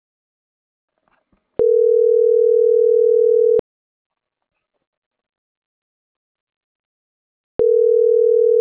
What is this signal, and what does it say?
Ringback tone (North America), 440 + 480 Hz.